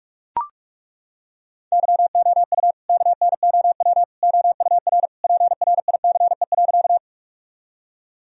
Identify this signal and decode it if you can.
Morse code, heard as a keyed tone, about 42 wpm, 700 Hz. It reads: YOU KNOW OUR PRICE.
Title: Sentences from Top 1000 Words - Review Part 4 - 45wpm